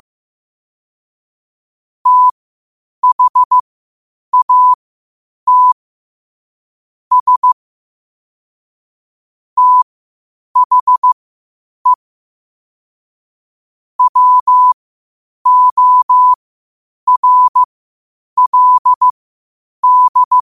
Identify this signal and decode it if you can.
Morse code, heard as a keyed tone, about 15 wpm, spaced as for 8 wpm, 1000 Hz. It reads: THAT S THE WORLD
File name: qotdfiles / 8wpm